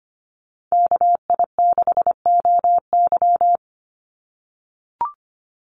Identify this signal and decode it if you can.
Morse code, heard as a keyed tone, about 25 wpm, 700 Hz. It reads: KI6OY
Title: Call Signs - 25wpm